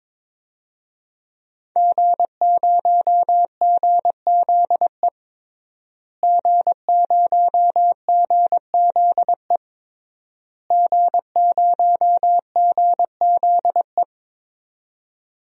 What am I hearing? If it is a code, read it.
Morse code, heard as a keyed tone, about 22 wpm, 700 Hz. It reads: G0GZE G0GZE G0GZE
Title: DXCC-100 Call Signs - 5 Characters - Repeated 3x - 22wpm